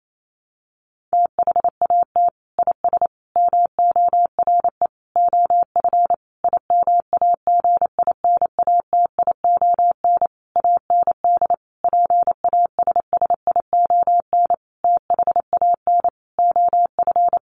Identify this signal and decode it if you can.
Morse code, heard as a keyed tone, about 28 wpm, 700 Hz. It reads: THAT IS MORE OF IMAGINATION AND PASSION THAN OF
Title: Sentences from Top 1300 Words 28wpm